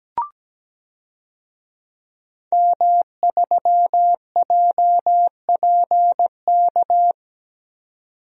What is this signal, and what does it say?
Morse code, heard as a keyed tone, about 17 wpm, 700 Hz. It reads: M3JPK
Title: DXCC-100 Call Signs - 5 Characters - 17wpm